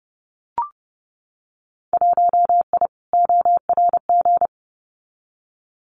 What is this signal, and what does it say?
Morse code, heard as a keyed tone, about 30 wpm, 700 Hz. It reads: 1I ORG
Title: Winter Field Day Exchanges Encore 30wpm